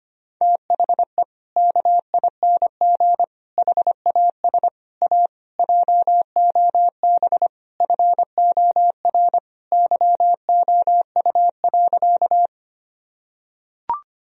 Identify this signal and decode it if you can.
Morse code, heard as a keyed tone, about 25 wpm, 700 Hz. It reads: THE KING HAS A JOB FOR YOU.
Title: Sentences from Top 1000 Words - Review Part 3 - 25wpm